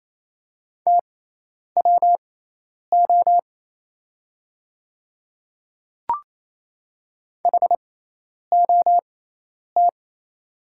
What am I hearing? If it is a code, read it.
Morse code, heard as a keyed tone, about 28 wpm, spaced as for 10 wpm, 700 Hz. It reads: TWO HOT
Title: 6x Character Spacing - Top 100 Words - 28wpm